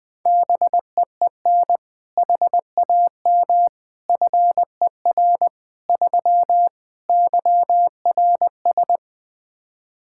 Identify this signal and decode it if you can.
Morse code, heard as a keyed tone, about 21 wpm, 700 Hz. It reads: BEEN HAM FER 3 YRS
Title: K1USN SST QSO Practice 20wpm